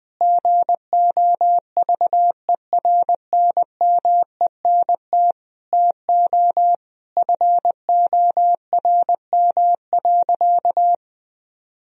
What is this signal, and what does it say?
Morse code, heard as a keyed tone, about 20 wpm, 700 Hz. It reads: GOVERNMENT TO FORM.
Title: Sentences from Top 1000 Words - Review Part 1 - 20wpm